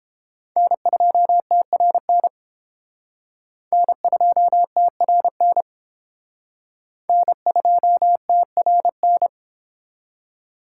Morse code, heard as a keyed tone, about 29 wpm, 700 Hz. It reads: N2TRN N2TRN N2TRN